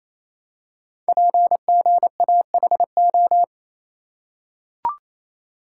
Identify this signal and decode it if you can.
Morse code, heard as a keyed tone, about 27 wpm, 700 Hz. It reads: PGAHO
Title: Code Groups - 5 Letters - 28wpm